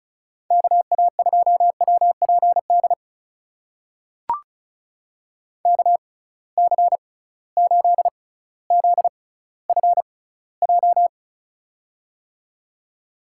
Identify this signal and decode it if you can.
Morse code, heard as a keyed tone, about 36 wpm, spaced as for 18 wpm, 700 Hz. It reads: KA2WPD KC8ZFJ
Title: USA Call Signs - 6 Characters - 6x Character Spacing - 35wpm